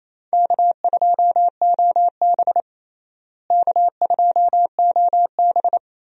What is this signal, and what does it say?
Morse code, heard as a keyed tone, about 28 wpm, 700 Hz. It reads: K2OB K2OB
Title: USA Call Signs - 4 Characters - Repeated 2x - 28wpm